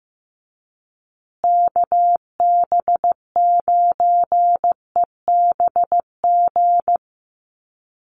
Morse code, heard as a keyed tone, about 15 wpm, 700 Hz. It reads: KB9EBG
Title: USA Call Signs Repeated 2x 15wpm